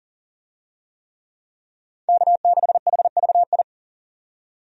Morse code, heard as a keyed tone, about 38 wpm, 700 Hz. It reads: K6HVI
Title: USA Call Signs - 5 Characters - 40wpm